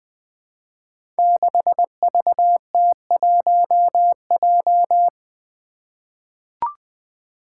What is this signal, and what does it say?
Morse code, heard as a keyed tone, about 20 wpm, 700 Hz. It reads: 6VT1J
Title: Five Challenging Letter-Number Combinations 20wpm